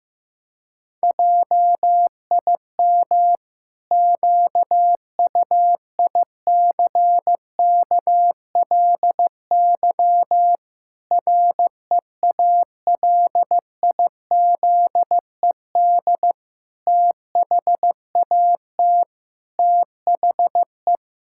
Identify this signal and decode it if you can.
Morse code, heard as a keyed tone, about 16 wpm, 700 Hz. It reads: JIM QUICKLY REALIZED THAT THE
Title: Pangram Challenge v2 - 15wpm